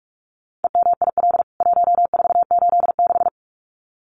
Morse code, heard as a keyed tone, about 44 wpm, 700 Hz. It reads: EMIL 1486
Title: CWT Exchange 45wpm